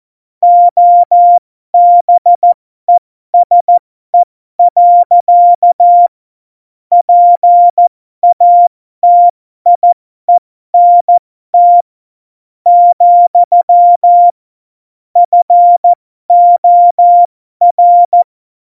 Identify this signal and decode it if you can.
Morse code, heard as a keyed tone, about 14 wpm, spaced as for 12 wpm, 700 Hz. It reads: OBESE. PATIENT, FOR
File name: Morsekurs / Witze